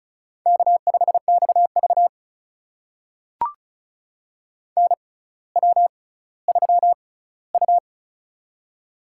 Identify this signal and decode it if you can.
Morse code, heard as a keyed tone, about 34 wpm, spaced as for 18 wpm, 700 Hz. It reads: K5XV NW3U
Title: USA Call Signs - 4 Characters - 6x Character Spacing - 35wpm